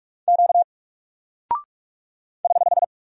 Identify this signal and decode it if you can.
Morse code, heard as a keyed tone, about 44 wpm, 700 Hz. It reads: Q <HH>
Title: Single Character (Letter:Number:Punctuation:Prosign) Alt 45wpm